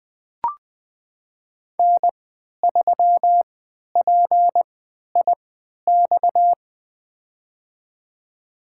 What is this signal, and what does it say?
Morse code, heard as a keyed tone, about 20 wpm, spaced as for 11 wpm, 700 Hz. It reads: N3PIX
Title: USA Call Signs - 5 Characters - 3x Character Spacing - 20wpm